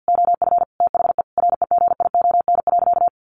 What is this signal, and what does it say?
Morse code, heard as a keyed tone, about 50 wpm, 700 Hz. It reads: OF THE REGION.
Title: Sentences from Top 1500 Words 50wpm